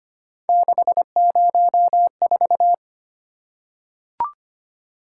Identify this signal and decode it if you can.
Morse code, heard as a keyed tone, about 25 wpm, 700 Hz. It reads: 604
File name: Three Letter-Number Combinations - 25wpm